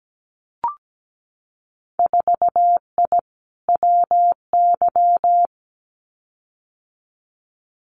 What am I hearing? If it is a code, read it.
Morse code, heard as a keyed tone, about 17 wpm, 700 Hz. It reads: 4I WY
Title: Winter Field Day Exchanges Encore 17wpm